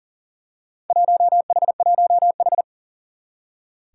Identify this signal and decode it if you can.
Morse code, heard as a keyed tone, about 40 wpm, 700 Hz. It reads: 1H1H
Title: Four Challenging Letter-Number Combinations 40wpm